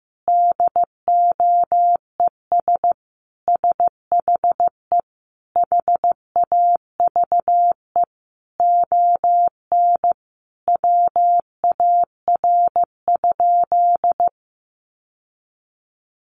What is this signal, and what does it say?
Morse code, heard as a keyed tone, about 15 wpm, 700 Hz. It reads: DOES SHE HAVE ON WAR?
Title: Sentences from Top 1200 Words 15wpm